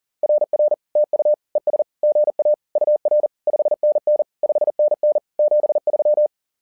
Morse code, heard as a keyed tone, about 40 wpm, 600 Hz. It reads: RR TU ES GA UR 5NN 5NN 73
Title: SOTA QSO Practice 40wpm